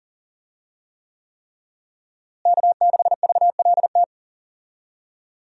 Morse code, heard as a keyed tone, about 37 wpm, 700 Hz. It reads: K6VLT
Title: USA Call Signs - 5 Characters - Repeated 3x - 40wpm